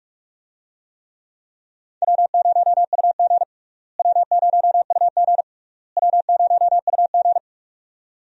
Morse code, heard as a keyed tone, about 43 wpm, 700 Hz. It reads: W0UG W0UG W0UG
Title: USA Call Signs - 4 Characters - Repeated 3x - 45wpm